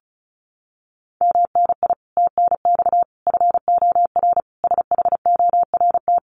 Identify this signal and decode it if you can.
Morse code, heard as a keyed tone, about 34 wpm, 700 Hz. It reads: MNI TNX FOR SHORT
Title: Common QSO Phrases 35wpm